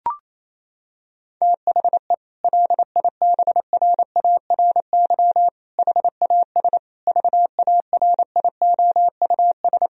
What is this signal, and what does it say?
Morse code, heard as a keyed tone, about 27 wpm, 700 Hz. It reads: THE LIBRARY HAS VARIOUS
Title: Sentences from Top 1000 Words - Review Part 4 - 28wpm